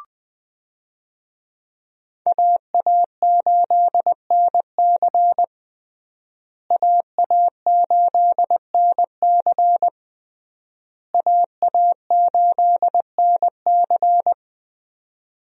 Morse code, heard as a keyed tone, about 20 wpm, 700 Hz. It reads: AA8NC AA8NC AA8NC
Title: USA Call Signs - 5 Characters - Repeated 3x - 20wpm